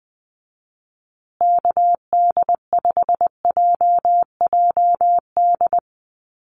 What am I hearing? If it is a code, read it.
Morse code, heard as a keyed tone, about 20 wpm, 700 Hz. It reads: KD5JJD
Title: USA Call Signs Repeated 3x 20wpm